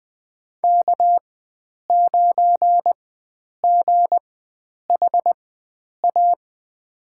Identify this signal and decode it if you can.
Morse code, heard as a keyed tone, about 20 wpm, spaced as for 9 wpm, 700 Hz. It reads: K9GHA
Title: USA Call Signs - 5 Characters - 4x Character Spacing - 20wpm